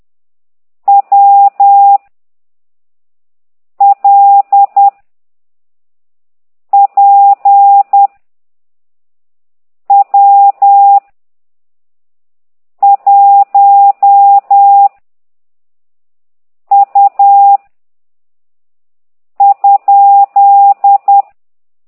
Morse code, heard as a keyed tone, about 10 wpm, 800 Hz. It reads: W L P W 1 U ?
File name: ct2idw / exerc15